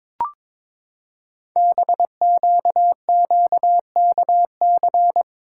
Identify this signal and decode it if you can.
Morse code, heard as a keyed tone, about 23 wpm, 700 Hz. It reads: BQQKC
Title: Code Groups - 5 Letters - 22wpm